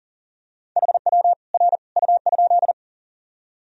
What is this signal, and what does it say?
Morse code, heard as a keyed tone, about 40 wpm, 700 Hz. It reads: HW R U?
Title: Sentences from Top 100 Words and Abbreviations 40wpm